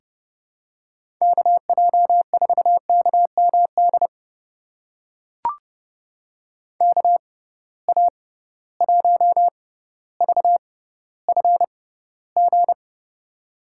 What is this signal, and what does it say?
Morse code, heard as a keyed tone, about 31 wpm, spaced as for 16 wpm, 700 Hz. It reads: KJ4KMD KA1VFG
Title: USA Call Signs - 6 Characters - 6x Character Spacing - 30wpm